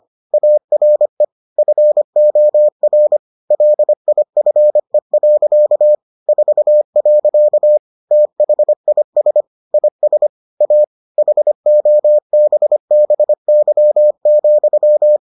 Morse code, heard as a keyed tone, about 25 wpm, 600 Hz. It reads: ARE FOR LIFE. 4. THIS IS A HOBBY,